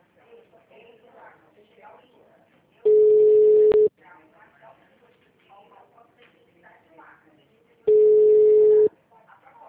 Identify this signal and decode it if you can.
Ringback tone (Europe, 425 Hz), with speech.
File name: out- -2020- - - .6595